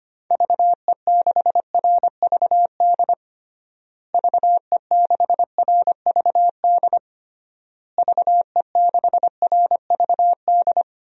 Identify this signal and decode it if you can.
Morse code, heard as a keyed tone, about 25 wpm, 700 Hz. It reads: VE6RVD VE6RVD VE6RVD
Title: DXCC-100 Call Signs - 6 Characters - Repeated 3x - 25wpm